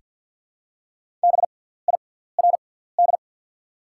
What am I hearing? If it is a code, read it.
Morse code, heard as a keyed tone, about 49 wpm, spaced as for 17 wpm, 690 Hz. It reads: BIRD
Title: 6x Character Spacing - Top 500 Words - 50wpm